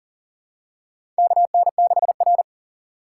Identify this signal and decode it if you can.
Morse code, heard as a keyed tone, about 40 wpm, 700 Hz. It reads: KN6R
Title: USA Call Signs - 4 Characters - 2x Character Spacing - 40wpm